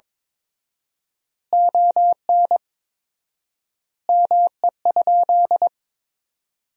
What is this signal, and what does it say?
Morse code, heard as a keyed tone, about 22 wpm, 700 Hz. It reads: ON ME?
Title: 4x Word-Spacing - Sentences from Top 500 Words - 22wpm